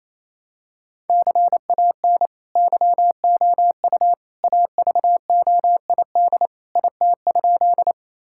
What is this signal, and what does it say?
Morse code, heard as a keyed tone, about 27 wpm, 700 Hz. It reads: CAN YOU AVOID IT?
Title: Sentences from Top 1000 Words - Review Part 6 - 28wpm